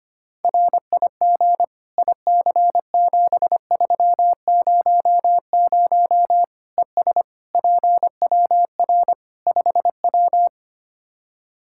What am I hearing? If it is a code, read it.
Morse code, heard as a keyed tone, about 25 wpm, 700 Hz. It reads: RIG IC7300 ES PWR 5W
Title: Practice QSO - Exchange 2 - Info - 25wpm